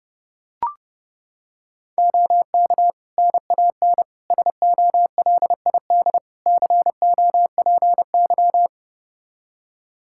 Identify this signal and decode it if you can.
Morse code, heard as a keyed tone, about 30 wpm, 700 Hz. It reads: OK NAN SOLID COPY
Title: Practice QSO - Exchange 3 - More Info - 30wpm